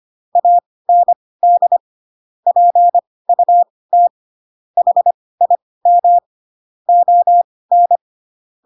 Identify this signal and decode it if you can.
Morse code, heard as a keyed tone, about 25 wpm, spaced as for 18 wpm, 700 Hz. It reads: AND PUT HIM ON
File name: Pinocchio ch3 12 25 ik1hge